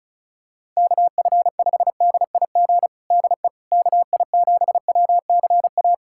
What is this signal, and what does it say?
Morse code, heard as a keyed tone, about 36 wpm, 690 Hz. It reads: KF5DIG DE KI7WCA
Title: USA Callsign DE Callsign 35wpm